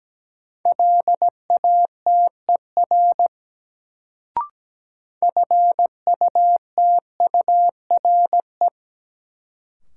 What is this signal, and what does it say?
Morse code, heard as a keyed tone, about 17 wpm, 700 Hz. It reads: LATER FUTURE
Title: Sentences from Top 700 Words - New Word Drill - 17wpm